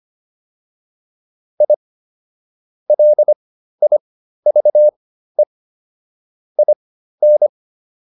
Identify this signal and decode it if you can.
Morse code, heard as a keyed tone, about 25 wpm, spaced as for 13 wpm, 600 Hz. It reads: I LIVE IN